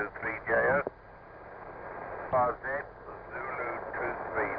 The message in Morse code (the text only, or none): none